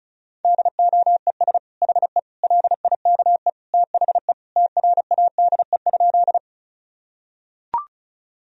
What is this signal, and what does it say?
Morse code, heard as a keyed tone, about 34 wpm, 700 Hz. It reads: DOES HE LIKE THE TRADE?
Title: Sentences from Top 1000 Words - Review Part 4 - 35wpm